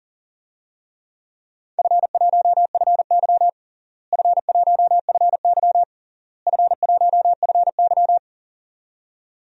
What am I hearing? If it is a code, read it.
Morse code, heard as a keyed tone, about 37 wpm, 700 Hz. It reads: F1FY F1FY F1FY
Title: DXCC-100 Call Signs - 4 Characters - Repeated 3x - 40wpm